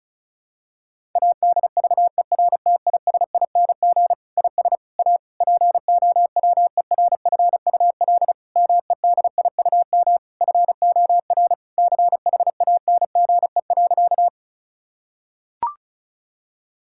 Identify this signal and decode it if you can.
Morse code, heard as a keyed tone, about 34 wpm, 700 Hz. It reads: ADVERTISING IS A POWERFUL MEDIUM FOR CHANGE.
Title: Sentences from Top 1000 Words - Review Part 6 - 35wpm